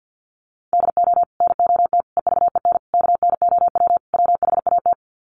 Morse code, heard as a keyed tone, about 50 wpm, 700 Hz. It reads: DO NOT EVEN KNOW WHAT
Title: Sets of 5 Words Encore 50wpm